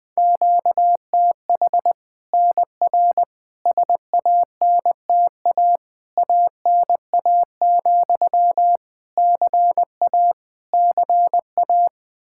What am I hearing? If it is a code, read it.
Morse code, heard as a keyed tone, about 21 wpm, 700 Hz. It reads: QTH NR SANTA ANA, CA CA
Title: Practice QSO - Exchange 1 - Beginning - 20wpm